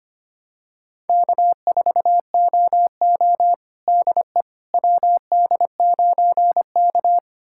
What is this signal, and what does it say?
Morse code, heard as a keyed tone, about 25 wpm, 700 Hz. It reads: K4OO DE WD9K
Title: USA Callsign DE Callsign - 4 Characters - 25wpm